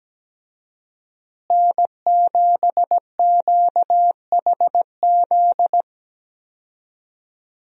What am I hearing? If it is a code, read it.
Morse code, heard as a keyed tone, about 17 wpm, 700 Hz. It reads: N7QHZ
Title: USA Call Signs - 5 Characters - Repeated 2x - 17wpm